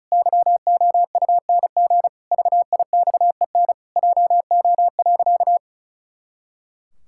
Morse code, heard as a keyed tone, about 36 wpm, 700 Hz. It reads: YOUNG VIXEN JO.
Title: Pangram Challenge v2 - 35wpm